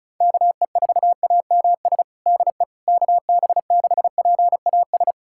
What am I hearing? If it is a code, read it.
Morse code, heard as a keyed tone, about 34 wpm, 700 Hz. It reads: KE4AMS DE KB6PAS
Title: USA Callsign DE Callsign 35wpm